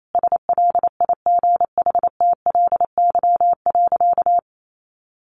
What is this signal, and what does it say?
Morse code, heard as a keyed tone, about 28 wpm, 700 Hz. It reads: SLIGHTLY.
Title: Sentences from Top 1300 Words 28wpm